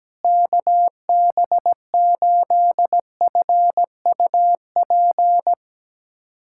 Morse code, heard as a keyed tone, about 17 wpm, 700 Hz. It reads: KB8FUP
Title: USA Callsign DE Callsign 17wpm